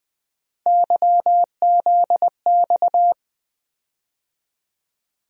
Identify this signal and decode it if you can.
Morse code, heard as a keyed tone, about 20 wpm, 700 Hz. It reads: YZX
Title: USA Call Sign Postfixes - 3 Letters 20wpm